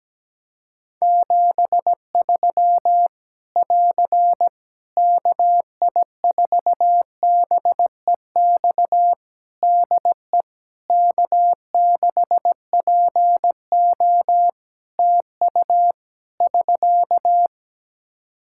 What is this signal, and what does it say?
Morse code, heard as a keyed tone, about 17 wpm, 700 Hz. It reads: 73 <AR> KI4BEX DE K6PO TU <SK>
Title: Practice QSO - Exchange 4 - Ending - 17wpm